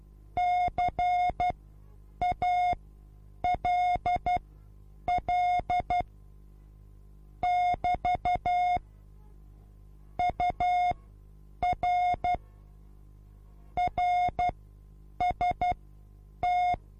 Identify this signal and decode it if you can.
Morse code, heard as a keyed tone, about 12 wpm, spaced as for 8 wpm, 720 Hz. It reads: CALL <BT> UR RST